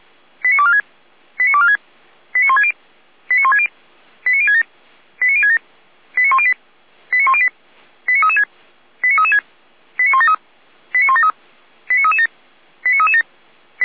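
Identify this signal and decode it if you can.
Five-tone selective call (ZVEI), heard as quick runs of five tones, tones of about 70 ms. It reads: 89246 89246 89160 89160 89067 89067 89108 89108 89336 89336 89173 89173 89337 89337
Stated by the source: ZVEI five-tone selective call 89078 received with an SDR.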